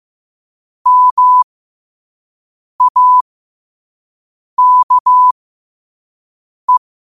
Morse code, heard as a keyed tone, about 15 wpm, spaced as for 5 wpm, 1000 Hz. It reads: MAKE
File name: qotdfiles / 5wpm